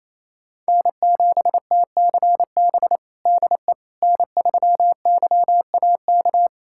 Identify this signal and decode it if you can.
Morse code, heard as a keyed tone, about 28 wpm, 700 Hz. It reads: N7TCB DE N3YAK
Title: USA Callsign DE Callsign - 5 Characters - 28wpm